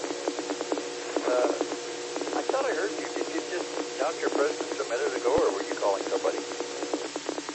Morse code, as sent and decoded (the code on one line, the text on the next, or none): none